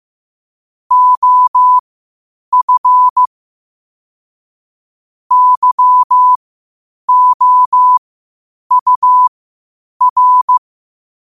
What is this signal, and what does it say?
Morse code, heard as a keyed tone, about 15 wpm, spaced as for 8 wpm, 1000 Hz. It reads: OF YOUR
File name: qotdfiles / 8wpm